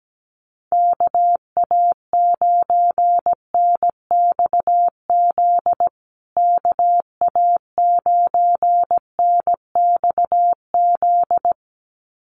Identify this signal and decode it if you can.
Morse code, heard as a keyed tone, about 17 wpm, 700 Hz. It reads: KA9NXZ KA9NXZ
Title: USA Call Signs Repeated 2x 17wpm